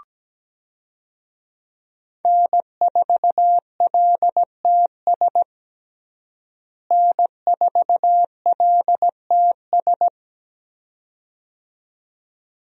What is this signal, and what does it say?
Morse code, heard as a keyed tone, about 17 wpm, 700 Hz. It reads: N4LTS N4LTS
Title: USA Call Signs - 5 Characters - Repeated 2x - 17wpm